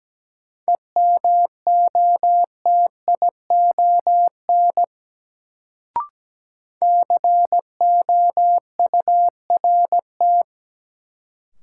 Morse code, heard as a keyed tone, about 17 wpm, 700 Hz. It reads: EMOTION COURT
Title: Sentences from Top 1600 Words - New Word Drill - 17wpm